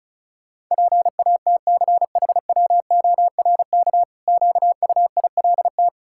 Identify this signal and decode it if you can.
Morse code, heard as a keyed tone, about 34 wpm, 700 Hz. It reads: PATCHWORK QUILT
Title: Pangram Challenge v2 - 35wpm